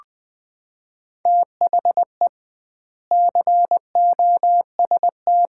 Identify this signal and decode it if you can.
Morse code, heard as a keyed tone, about 21 wpm, 700 Hz. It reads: THE COST
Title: 2x Word-Spacing - Sets of 3 Words - 20wpm